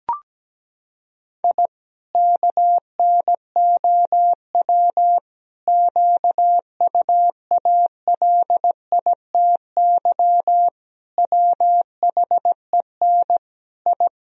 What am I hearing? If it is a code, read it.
Morse code, heard as a keyed tone, about 17 wpm, 700 Hz. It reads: I KNOW QUALITY WHEN I
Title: Sentences from Top 1000 Words - Review Part 4 - 17wpm